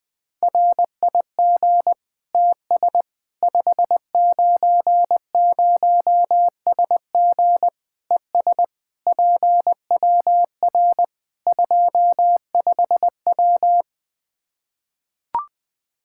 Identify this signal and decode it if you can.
Morse code, heard as a keyed tone, about 20 wpm, 700 Hz. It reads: RIG TS 590SG ES PWR 25W
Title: Practice QSO - Exchange 2 - Info - 20wpm